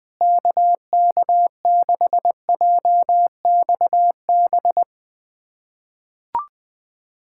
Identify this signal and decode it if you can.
Morse code, heard as a keyed tone, about 20 wpm, 700 Hz. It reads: KK6JXB
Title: USA Callsign DE Callsign 20wpm